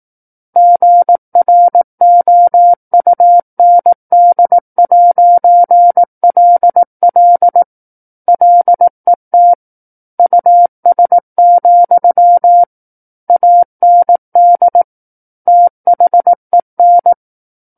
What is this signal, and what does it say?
Morse code, heard as a keyed tone, about 18 wpm, 700 Hz. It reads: GROUND'LL LET US, AND THEN
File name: 18 WPM